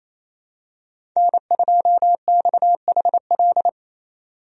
Morse code, heard as a keyed tone, about 28 wpm, 700 Hz. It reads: N2XHL
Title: USA Call Signs - 5 Characters - Repeated 2x - 28wpm